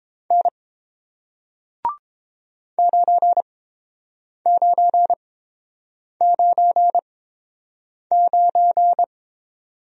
Morse code, heard as a keyed tone, about 29 wpm, spaced as for 7 wpm, 700 Hz. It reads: N 9999